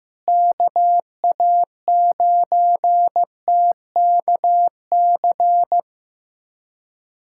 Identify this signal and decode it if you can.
Morse code, heard as a keyed tone, about 15 wpm, 700 Hz. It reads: KA9TKC
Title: USA Callsign DE Callsign 15wpm